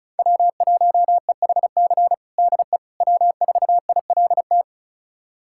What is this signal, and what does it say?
Morse code, heard as a keyed tone, about 34 wpm, 700 Hz. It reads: W1EHC DE W4ILT
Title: USA Callsign DE Callsign - 5 Characters - 35wpm